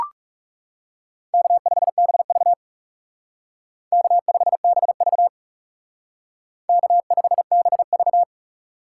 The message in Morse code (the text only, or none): K5BV K5BV K5BV